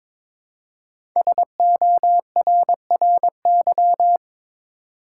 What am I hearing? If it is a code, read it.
Morse code, heard as a keyed tone, about 22 wpm, 700 Hz. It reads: SORRY
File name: 5 Letter Words - First 3 Letters Given - 22wpm